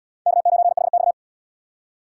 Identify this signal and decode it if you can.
Morse code, heard as a keyed tone, about 69 wpm, 690 Hz. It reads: K9HX